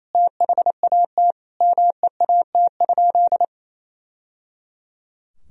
Morse code, heard as a keyed tone, about 27 wpm, 700 Hz. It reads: THAT MEAT?
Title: Sentences from Top 1000 Words - Review Part 3 - 28wpm